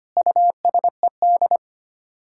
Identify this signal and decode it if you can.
Morse code, heard as a keyed tone, about 25 wpm, 700 Hz. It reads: USED
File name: Sets of 9 Words - 25wpm